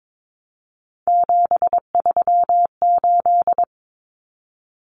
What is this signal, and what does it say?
Morse code, heard as a keyed tone, about 22 wpm, 700 Hz. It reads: 738
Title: Numbers - 3-Digit 22wpm